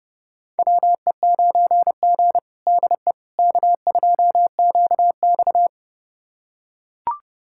Morse code, heard as a keyed tone, about 30 wpm, 700 Hz. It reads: WE9G DE K2QX
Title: USA Callsign DE Callsign - 4 Characters - 30wpm